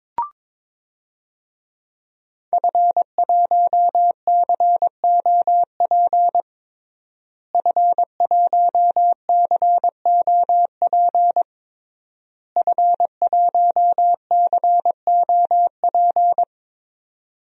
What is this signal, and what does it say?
Morse code, heard as a keyed tone, about 22 wpm, 700 Hz. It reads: F1COP F1COP F1COP
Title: DXCC-100 Call Signs - 5 Characters - Repeated 3x - 22wpm